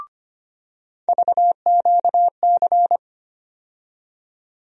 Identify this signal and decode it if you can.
Morse code, heard as a keyed tone, about 25 wpm, 700 Hz. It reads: VQC
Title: USA Call Sign Postfixes - 3 Letters 25wpm